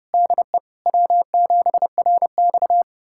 Morse code, heard as a keyed tone, about 32 wpm, 700 Hz. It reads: DE W7RX
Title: USA Callsign DE Callsign - 4 Characters - 30wpm